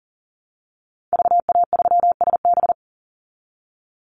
Morse code, heard as a keyed tone, about 40 wpm, 700 Hz. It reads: VA3SB
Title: Call Signs Repeated 2x - 40wpm